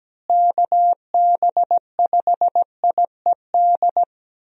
Morse code, heard as a keyed tone, about 17 wpm, 700 Hz. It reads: KB5IED
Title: USA Call Signs - 6 Characters - Repeated 2x - 17wpm